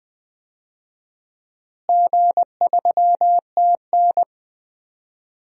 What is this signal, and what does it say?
Morse code, heard as a keyed tone, about 20 wpm, 700 Hz. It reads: G3TN
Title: DXCC-100 Call Signs - 4 Characters - Repeated 3x - 20wpm